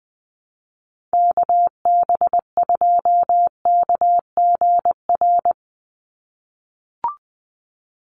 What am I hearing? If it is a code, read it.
Morse code, heard as a keyed tone, about 20 wpm, 700 Hz. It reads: KB2KGR
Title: USA Call Signs 20wpm